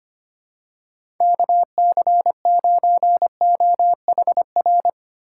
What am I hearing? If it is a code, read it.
Morse code, heard as a keyed tone, about 25 wpm, 700 Hz. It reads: KC9OHR
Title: USA Call Signs - 4x Character Spacing - 25wpm